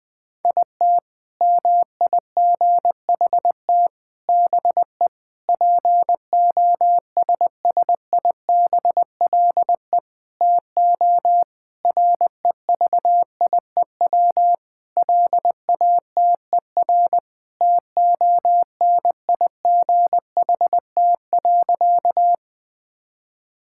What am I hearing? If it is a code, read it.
Morse code, heard as a keyed tone, about 20 wpm, 700 Hz. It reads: IT MIGHT BE POSSIBLE TO REVIEW LATER TONIGHT.
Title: Sentences from Top 1000 Words - Review Part 6 - 20wpm